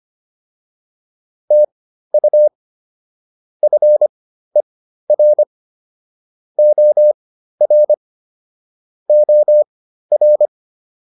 Morse code, heard as a keyed tone, about 25 wpm, spaced as for 13 wpm, 600 Hz. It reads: TU FER OR OR